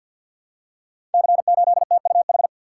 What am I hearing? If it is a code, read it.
Morse code, heard as a keyed tone, about 52 wpm, 690 Hz. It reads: C8TUH
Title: Code Groups - 5 Letter:Numbers - 50wpm